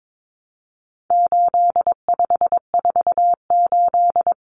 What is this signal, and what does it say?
Morse code, heard as a keyed tone, about 22 wpm, 700 Hz. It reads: 8548
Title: Numbers - 4-Digit 22wpm